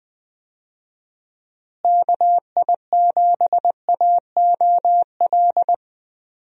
Morse code, heard as a keyed tone, about 20 wpm, 700 Hz. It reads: KI7AOL
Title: USA Call Signs - 6 Characters - Repeated 3x - 20wpm